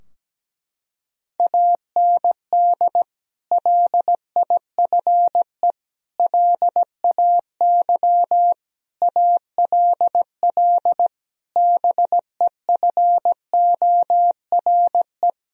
Morse code, heard as a keyed tone, about 17 wpm, 700 Hz. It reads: AND LIFE LAY ALL BEFORE
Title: Sentences from Top 1000 Words - Review Part 2 - 17wpm